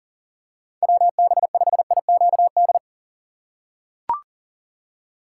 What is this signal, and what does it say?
Morse code, heard as a keyed tone, about 40 wpm, 700 Hz. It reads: WB5IQD
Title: USA Call Signs - 6 Characters - 6x Character Spacing - 40wpm